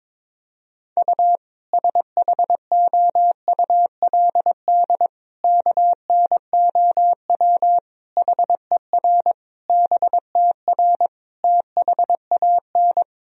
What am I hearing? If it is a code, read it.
Morse code, heard as a keyed tone, about 22 wpm, 700 Hz. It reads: U SHOULD KNOW HER BTR THAN
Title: Sentences from Top 400 Words and Abbreviations 22wpm